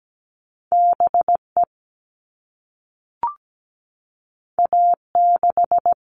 Morse code, heard as a keyed tone, about 17 wpm, 700 Hz. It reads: BE A6
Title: All Cut Numbers - 2-Digit 17wpm